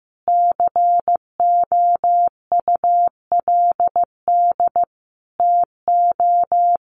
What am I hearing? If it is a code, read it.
Morse code, heard as a keyed tone, about 15 wpm, 700 Hz. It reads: COULD TO
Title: Sentences from Top 1500 Words 15wpm